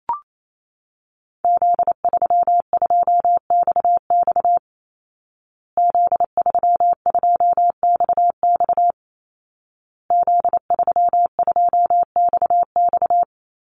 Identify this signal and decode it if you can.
Morse code, heard as a keyed tone, about 28 wpm, 700 Hz. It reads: Z32XX Z32XX Z32XX
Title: Call Signs Repeated 2x - 28wpm